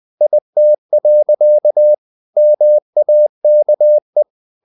Morse code, heard as a keyed tone, about 20 wpm, 600 Hz. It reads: IT. MAKE
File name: the old man and the sea part 2 20wpm